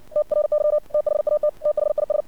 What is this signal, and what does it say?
Morse code, heard as a keyed tone, about 60 wpm, 600 Hz. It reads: TRY THAT THIS
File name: archived / CRSnet